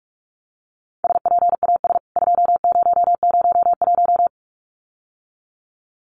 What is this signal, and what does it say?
Morse code, heard as a keyed tone, about 44 wpm, 700 Hz. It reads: SPAS 2001
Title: CWT Exchange 45wpm